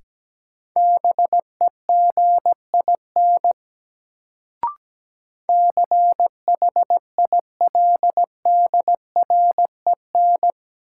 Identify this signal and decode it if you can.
Morse code, heard as a keyed tone, about 17 wpm, 700 Hz. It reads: BEGIN CHILDREN
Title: Sentences from Top 300 Words - New Word Drill - 17wpm